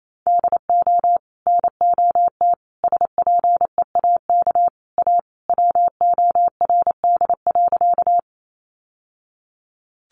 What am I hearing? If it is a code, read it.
Morse code, heard as a keyed tone, about 28 wpm, 700 Hz. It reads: DO NOT SPEAK A WORD.
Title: Sentences from Top 1100 Words 28wpm